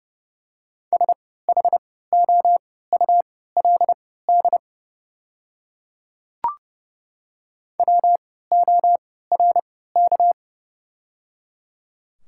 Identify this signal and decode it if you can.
Morse code, heard as a keyed tone, about 30 wpm, spaced as for 17 wpm, 700 Hz. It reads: SHOULD WORK
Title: 3x Character Spacing - Top 300 Words - 30wpm